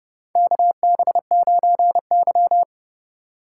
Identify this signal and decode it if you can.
Morse code, heard as a keyed tone, about 32 wpm, 700 Hz. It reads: KB9Y
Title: USA Call Signs - 4 Characters - 3x Character Spacing - 30wpm